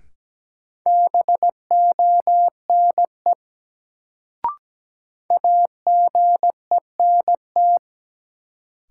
Morse code, heard as a keyed tone, about 17 wpm, 700 Hz. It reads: BONE AGENT
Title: Sentences from Top 1600 Words - New Word Drill - 17wpm